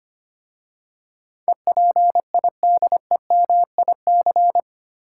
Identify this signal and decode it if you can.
Morse code, heard as a keyed tone, about 25 wpm, 700 Hz. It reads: EPIDEMIC
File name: 8 Letter Words - First 3 Letters Given - 25wpm